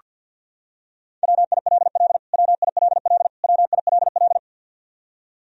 Morse code, heard as a keyed tone, about 49 wpm, 700 Hz. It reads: WILL WILL WILL
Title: Top 100 Words Repeated 50wpm